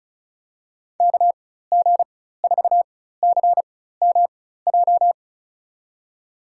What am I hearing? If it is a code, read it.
Morse code, heard as a keyed tone, about 36 wpm, spaced as for 16 wpm, 690 Hz. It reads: KG4CMJ
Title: USA Call Signs - 6 Characters - 4x Character Spacing - 35wpm